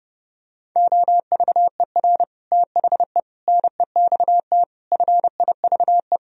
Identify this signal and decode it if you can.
Morse code, heard as a keyed tone, about 32 wpm, 700 Hz. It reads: OVER THE NEXT FIVE